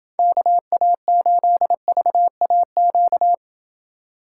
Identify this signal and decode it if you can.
Morse code, heard as a keyed tone, about 29 wpm, 700 Hz. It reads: KA8VAQ